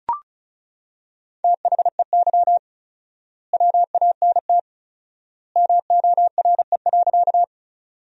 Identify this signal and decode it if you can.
Morse code, heard as a keyed tone, about 36 wpm, 700 Hz. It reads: THEY WANT MORE.
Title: 4x Word-Spacing - Sentences from Top 200 Words - 35wpm